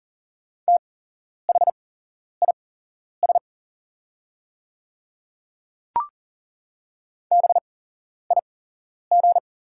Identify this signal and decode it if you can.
Morse code, heard as a keyed tone, about 40 wpm, spaced as for 11 wpm, 690 Hz. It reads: THIS BIG
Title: 8x Character Spacing - Top 200 Words - 40wpm